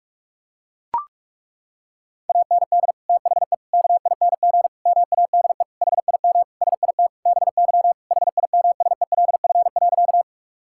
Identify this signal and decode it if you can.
Morse code, heard as a keyed tone, about 41 wpm, 700 Hz. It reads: AND THE KING MADE HIM SIT BY HIMSELF.
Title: Sentences from Top 1600 Words 45wpm